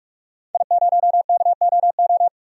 Morse code, heard as a keyed tone, about 45 wpm, 700 Hz. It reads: I0KOO
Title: DXCC-100 Call Signs - 5 Characters - Repeated 3x - 45wpm